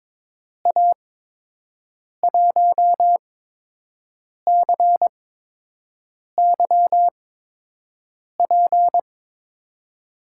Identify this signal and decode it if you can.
Morse code, heard as a keyed tone, about 22 wpm, spaced as for 6 wpm, 700 Hz. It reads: A1CYP
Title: USA Call Signs - 8x Character Spacing - 22wpm